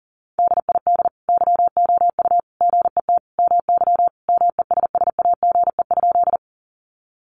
Morse code, heard as a keyed tone, about 40 wpm, 700 Hz. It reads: DID YOU GET MY MESSAGE?
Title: Sentences from Top 1100 Words 40wpm